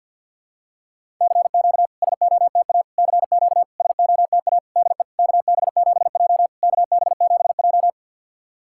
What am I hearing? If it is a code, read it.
Morse code, heard as a keyed tone, about 49 wpm, 690 Hz. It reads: CQ SOTA CQ SOTA DE KB7J KB7J
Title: SOTA QSO Practice 50wpm